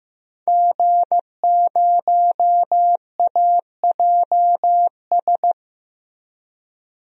Morse code, heard as a keyed tone, about 15 wpm, 700 Hz. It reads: G0AJS
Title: DXCC-100 Call Signs - 5 Characters - Repeated 3x - 15wpm